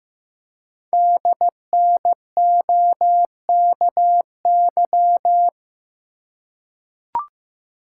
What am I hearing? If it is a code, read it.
Morse code, heard as a keyed tone, about 15 wpm, 700 Hz. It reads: DNOKY
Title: Code Groups - 5 Letters - 15wpm